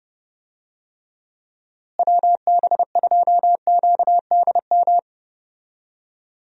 Morse code, heard as a keyed tone, about 29 wpm, 700 Hz. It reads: WB2QDM
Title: USA Call Signs - 6 Characters - Repeated 3x - 30wpm